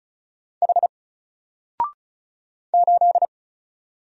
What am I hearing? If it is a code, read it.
Morse code, heard as a keyed tone, about 36 wpm, 700 Hz. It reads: H 8
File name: Single-Letter-Number-35wpm